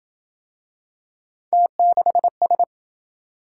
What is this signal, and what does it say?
Morse code, heard as a keyed tone, about 27 wpm, 700 Hz. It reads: T6S